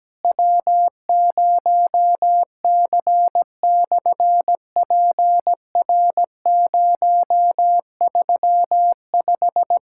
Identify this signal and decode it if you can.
Morse code, heard as a keyed tone, about 17 wpm, 700 Hz. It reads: W0C/PR035
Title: SOTA QSO Practice 17wpm